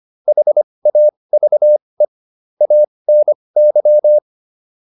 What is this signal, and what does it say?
Morse code, heard as a keyed tone, about 25 wpm, spaced as for 20 wpm, 600 Hz. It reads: HAVE ANY
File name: WD405 20